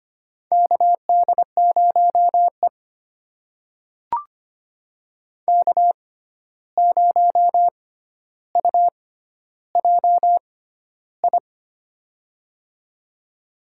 Morse code, heard as a keyed tone, about 25 wpm, spaced as for 12 wpm, 700 Hz. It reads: KD0E K0UJI
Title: USA Call Signs - 6x Character Spacing - 25wpm